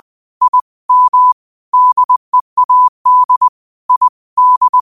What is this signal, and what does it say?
Morse code, heard as a keyed tone, about 20 wpm, 1000 Hz. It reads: I M DEAD I D
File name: qotdfiles / 20wpm